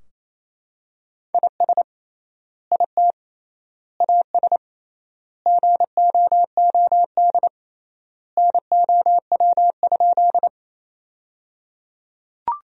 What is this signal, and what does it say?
Morse code, heard as a keyed tone, about 28 wpm, 700 Hz. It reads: IS IT AS GOOD NOW?
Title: 3x Word-Spacing - Sentences from Top 200 Words - 28wpm